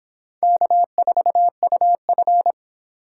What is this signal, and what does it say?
Morse code, heard as a keyed tone, about 30 wpm, 700 Hz. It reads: K4UF